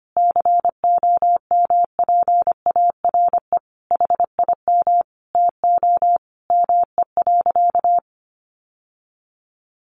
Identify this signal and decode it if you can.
Morse code, heard as a keyed tone, about 25 wpm, 700 Hz. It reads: COMPARE HIM TO ME.
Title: Sentences from Top 1300 Words 25wpm